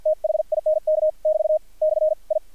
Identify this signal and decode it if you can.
Morse code, heard as a keyed tone, about 55 wpm, spaced as for 47 wpm, 620 Hz, with radio noise. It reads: THINK <BT> / I
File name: archived / CRSnet